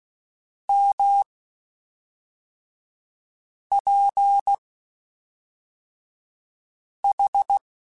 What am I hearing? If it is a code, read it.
Morse code, heard as a keyed tone, about 16 wpm, spaced as for 3 wpm, 800 Hz. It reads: MPH
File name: Spaced Characters / Spaced Characters-mp3s / Weather 4 10 sp @5